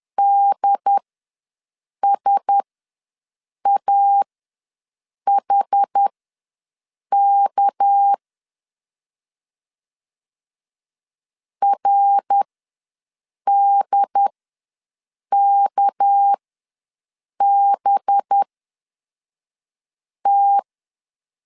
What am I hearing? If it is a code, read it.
Morse code, heard as a keyed tone, about 10 wpm, spaced as for 6 wpm, 800 Hz. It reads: DSAHK RDKBT
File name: LEZIONI / 6 PKX0 / 6 PK ws 10wpm 3745